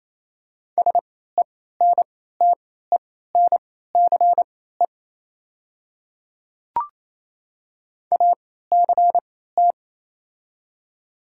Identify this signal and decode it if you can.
Morse code, heard as a keyed tone, about 27 wpm, spaced as for 16 wpm, 690 Hz. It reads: SENTENCE ACT
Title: 3x Character Spacing - Top 300 Words - 28wpm